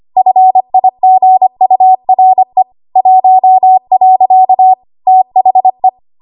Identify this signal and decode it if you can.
Morse code, heard as a keyed tone, about 25 wpm, 750 Hz. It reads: FIGURE 1. THE
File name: Archive / 25 WPM / 25WPM